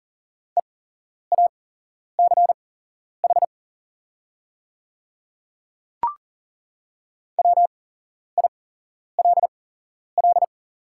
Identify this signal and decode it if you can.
Morse code, heard as a keyed tone, about 39 wpm, spaced as for 11 wpm, 700 Hz. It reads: EACH WILL
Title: 8x Character Spacing - Top 100 Words - 40wpm